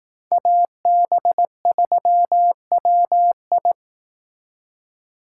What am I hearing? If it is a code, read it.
Morse code, heard as a keyed tone, about 18 wpm, 700 Hz. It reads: AB3WI
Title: Speed Racing - USA 2x2 Call Signs 15wpm